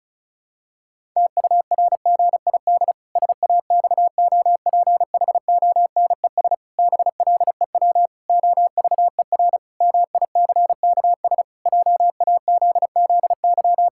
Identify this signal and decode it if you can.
Morse code, heard as a keyed tone, about 36 wpm, 700 Hz. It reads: TURGID SAXOPHONES BLEW OVER MICKS JAZZY
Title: Pangram Challenge v2 - 35wpm